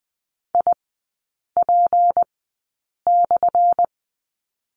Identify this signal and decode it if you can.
Morse code, heard as a keyed tone, about 20 wpm, spaced as for 8 wpm, 700 Hz. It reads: IP/
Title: 178 Lesson 29e - Slash - Three Characters 20wpm